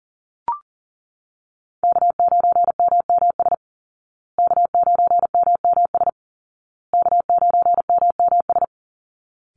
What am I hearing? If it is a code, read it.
Morse code, heard as a keyed tone, about 40 wpm, 700 Hz. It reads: K9MMS K9MMS K9MMS
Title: Call Signs Repeated 2x - 40wpm